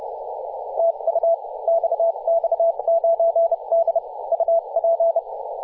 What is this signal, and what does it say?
Morse code, heard as a keyed tone, about 30 wpm, spaced as for 26 wpm, 670 Hz, with radio noise. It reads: TU XX9D UP
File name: posts / 2017-feb-16-0552z-xx9d-21-026mhz-cw